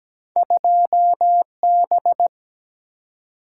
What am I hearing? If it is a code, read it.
Morse code, heard as a keyed tone, about 17 wpm, 700 Hz. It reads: 2B
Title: Single-Letter Single-Number 17wpm